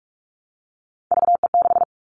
Morse code, heard as a keyed tone, about 46 wpm, 700 Hz. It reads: VE6
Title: Callsign Prefixes for Top 100 DXCC - 45wpm